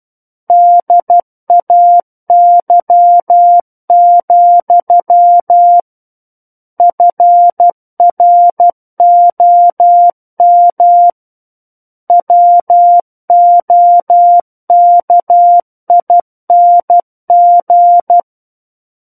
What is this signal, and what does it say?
Morse code, heard as a keyed tone, about 12 wpm, 700 Hz. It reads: DAY, FROM WOKING
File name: Book One / 03-Chapter 3 - 12 WPM